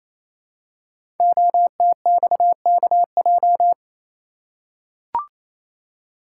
Morse code, heard as a keyed tone, about 29 wpm, 700 Hz. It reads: OTXKJ
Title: Code Groups - 5 Letter:Numbers - 28wpm